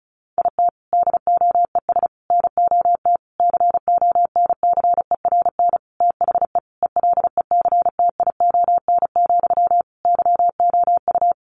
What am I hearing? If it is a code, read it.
Morse code, heard as a keyed tone, about 34 wpm, 700 Hz. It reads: IT DOES NOT CONCERN THE ELECTION, YOU
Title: Sentences from Top 1500 Words 35wpm